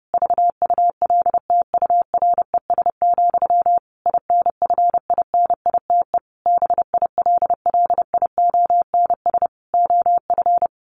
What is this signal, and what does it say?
Morse code, heard as a keyed tone, about 30 wpm, 700 Hz. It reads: VULTURES, INFINITE BILLIONS OF
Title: A Sound Of Thunder 30wpm